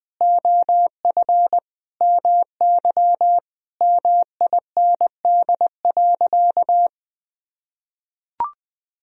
Morse code, heard as a keyed tone, about 20 wpm, 700 Hz. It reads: OF MY MIND.
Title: Sentences from Top 1000 Words - Review Part 2 - 20wpm